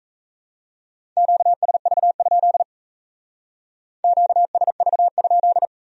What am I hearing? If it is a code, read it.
Morse code, heard as a keyed tone, about 35 wpm, 700 Hz. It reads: QSV? QSV?